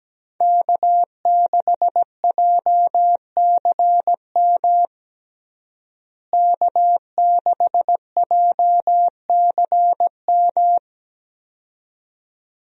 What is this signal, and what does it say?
Morse code, heard as a keyed tone, about 17 wpm, 700 Hz. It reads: K6JCM K6JCM
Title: USA Call Signs - 5 Characters - Repeated 2x - 17wpm